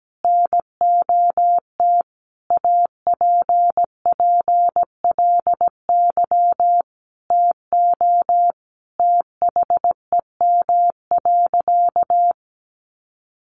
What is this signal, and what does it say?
Morse code, heard as a keyed tone, about 17 wpm, 700 Hz. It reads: NOT APPLY TO THEM.
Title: Sentences from Top 1200 Words 17wpm